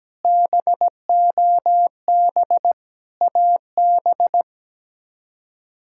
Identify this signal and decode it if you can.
Morse code, heard as a keyed tone, about 17 wpm, 700 Hz. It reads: BOB AB
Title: K1USN SST QSO Practice - 17wpm